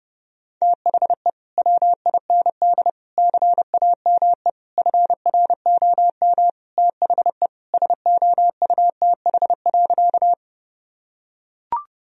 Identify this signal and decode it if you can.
Morse code, heard as a keyed tone, about 30 wpm, 700 Hz. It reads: THE WIND CAME FROM THE SOUTH.
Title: Sentences from Top 1000 Words - Review Part 2 - 30wpm